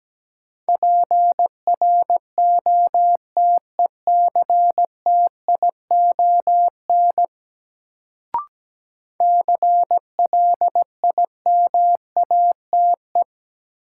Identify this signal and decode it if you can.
Morse code, heard as a keyed tone, about 17 wpm, 700 Hz. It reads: PROTECTION CLIMATE
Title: Sentences from Top 1600 Words - New Word Drill - 17wpm